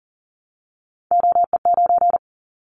Morse code, heard as a keyed tone, about 40 wpm, 700 Hz. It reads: OE9
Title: Callsign Prefixes for Top 25 DXCC - 40wpm